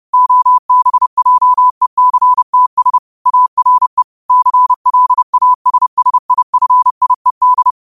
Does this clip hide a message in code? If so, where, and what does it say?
Yes: Morse code from 0.1 to 7.7 s, heard as a keyed tone, about 30 wpm, 1000 Hz. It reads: OBJECTS ARE CLASSIFIED